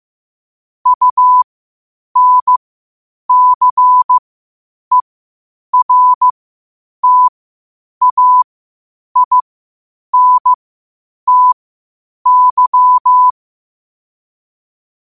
Morse code, heard as a keyed tone, about 15 wpm, spaced as for 8 wpm, 1000 Hz. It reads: UNCERTAINTY
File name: qotdfiles / 8wpm